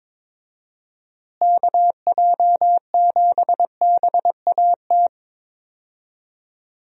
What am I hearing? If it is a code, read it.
Morse code, heard as a keyed tone, about 22 wpm, 700 Hz. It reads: KJ7BAT
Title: USA Call Signs - 6 Characters - Repeated 3x - 22wpm